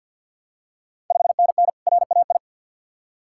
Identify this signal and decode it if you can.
Morse code, heard as a keyed tone, about 52 wpm, 700 Hz. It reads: 5NN RAI
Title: Ontario QSO Party Exchange 50wpm